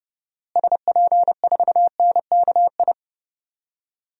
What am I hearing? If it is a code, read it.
Morse code, heard as a keyed tone, about 31 wpm, 700 Hz. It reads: SP4NKI
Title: DXCC-100 Call Signs - 6 Characters - Repeated 2x - 30wpm